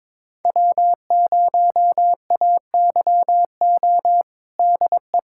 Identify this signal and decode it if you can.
Morse code, heard as a keyed tone, about 22 wpm, 700 Hz. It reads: W0AYO DE
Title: USA Callsign DE Callsign - 5 Characters - 22wpm